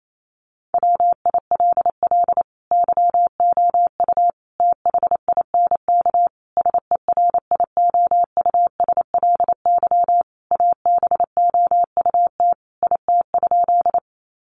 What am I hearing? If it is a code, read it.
Morse code, heard as a keyed tone, about 28 wpm, 700 Hz. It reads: WILL YOU THINK SERIOUSLY ABOUT IT?
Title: Sentences from Top 1200 Words 28wpm